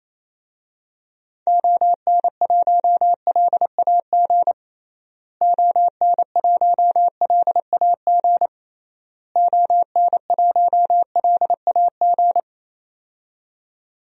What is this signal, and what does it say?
Morse code, heard as a keyed tone, about 28 wpm, 700 Hz. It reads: ON1LAG ON1LAG ON1LAG
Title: DXCC-100 Call Signs - 6 Characters - Repeated 3x - 28wpm